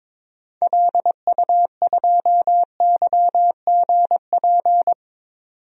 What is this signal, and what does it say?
Morse code, heard as a keyed tone, about 22 wpm, 700 Hz. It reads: LU2YGP
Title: DXCC-100 Call Signs - 6 Characters - 22wpm